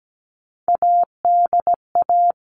Morse code, heard as a keyed tone, about 17 wpm, 700 Hz. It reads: ADA
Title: All Cut Numbers - 3-Digit 17wpm